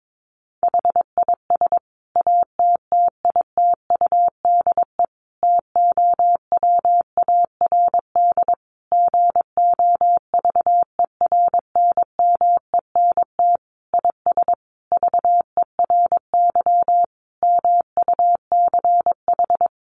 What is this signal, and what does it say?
Morse code, heard as a keyed tone, about 22 wpm, 700 Hz. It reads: HIS ATTITUDE TOWARD GOVERNMENT IS VERY MUCH
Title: Sentences from Top 1300 Words 22wpm